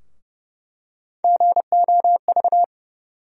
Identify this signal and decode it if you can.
Morse code, heard as a keyed tone, about 31 wpm, 700 Hz. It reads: GOV
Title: USA Call Sign Postfixes - 3 Letters 30wpm